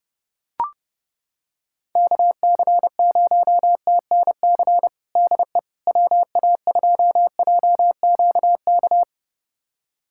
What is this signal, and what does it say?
Morse code, heard as a keyed tone, about 32 wpm, 700 Hz. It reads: KC0TNC DE WA2JQK
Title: USA Callsign DE Callsign - 6 Characters - 30wpm